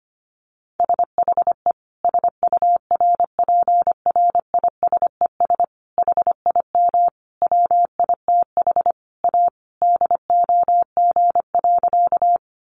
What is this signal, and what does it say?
Morse code, heard as a keyed tone, about 25 wpm, 700 Hz. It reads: SHE SURPRISES HIM WITH A DOG.
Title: Additional Easy Phrases 25wpm